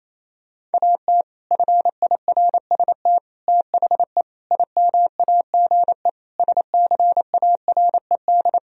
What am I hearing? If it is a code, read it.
Morse code, heard as a keyed tone, about 28 wpm, 700 Hz. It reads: AT FIRST THE IMAGE SCARED
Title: Sentences from Top 1000 Words - Review Part 4 - 28wpm